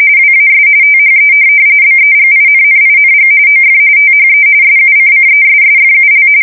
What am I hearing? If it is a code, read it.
RTTY, about 45.45 baud. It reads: WAS THE BEGINNING OF THE SUCCESSFUL
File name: fsk170 / USS-New-Jersey